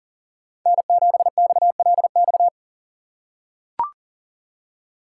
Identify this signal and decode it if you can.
Morse code, heard as a keyed tone, about 40 wpm, 700 Hz. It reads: N7XLX
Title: USA Call Signs - 5 Characters - 4x Character Spacing - 40wpm